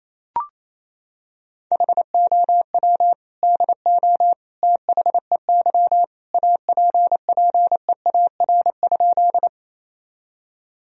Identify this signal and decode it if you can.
Morse code, heard as a keyed tone, about 28 wpm, 700 Hz. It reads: HOW DO THEY APPEAR?
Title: Sentences from Top 1000 Words - Review Part 2 - 28wpm